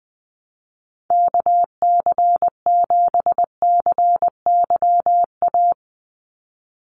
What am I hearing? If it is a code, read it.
Morse code, heard as a keyed tone, about 20 wpm, 700 Hz. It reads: KC7CYA
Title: USA Call Signs 20wpm